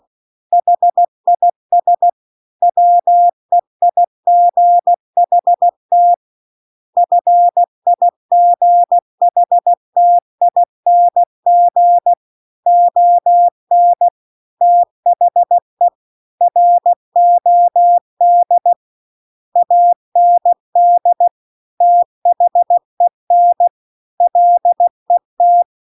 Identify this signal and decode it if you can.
Morse code, heard as a keyed tone, about 16 wpm, 700 Hz. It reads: HIS WEIGHT FIGHTING ON THE ROD AND THEN LET